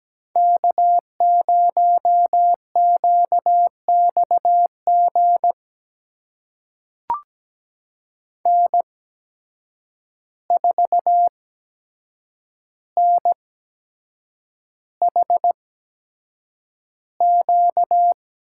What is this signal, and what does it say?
Morse code, heard as a keyed tone, about 17 wpm, spaced as for 7 wpm, 700 Hz. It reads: K0QXG N4NHQ
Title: USA Call Signs - 8x Character Spacing - 17wpm